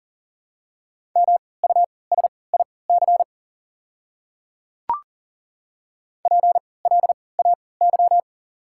Morse code, heard as a keyed tone, about 40 wpm, spaced as for 23 wpm, 690 Hz. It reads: MUSIC PLAY
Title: 3x Character Spacing - Top 300 Words - 40wpm